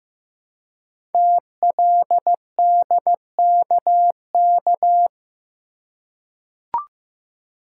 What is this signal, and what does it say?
Morse code, heard as a keyed tone, about 15 wpm, 700 Hz. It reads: TLDKK
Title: Code Groups - 5 Letters - 15wpm